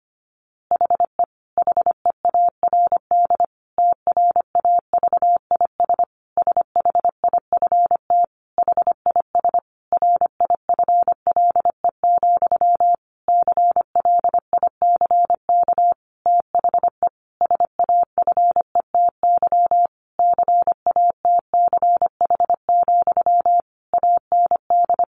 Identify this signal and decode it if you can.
Morse code, heard as a keyed tone, about 25 wpm, 700 Hz. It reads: HE HEARD TRAVIS SHIFT HIS RIFLE, CLICK THE SAFETY CATCH, AND
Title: A Sound Of Thunder 25wpm